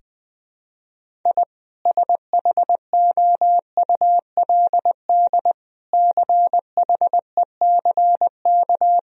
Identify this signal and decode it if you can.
Morse code, heard as a keyed tone, about 20 wpm, 700 Hz. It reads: I SHOULD CHECK
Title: Sentences from Top 1000 Words - Review Part 3 - 20wpm